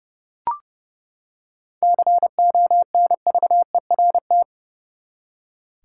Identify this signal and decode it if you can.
Morse code, heard as a keyed tone, about 30 wpm, 700 Hz. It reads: CONVERT
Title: Seven Letter Words - 30wpm